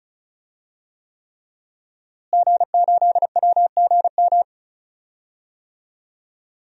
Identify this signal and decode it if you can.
Morse code, heard as a keyed tone, about 36 wpm, 700 Hz. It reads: G8WGM
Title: DXCC-100 Call Signs - 5 Characters - Repeated 3x - 35wpm